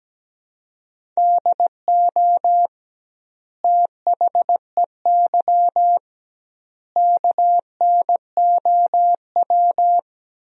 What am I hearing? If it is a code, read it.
Morse code, heard as a keyed tone, about 17 wpm, 700 Hz. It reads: DO THEY KNOW
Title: 2x Word-Spacing - Sentences from Top 100 Words - 17wpm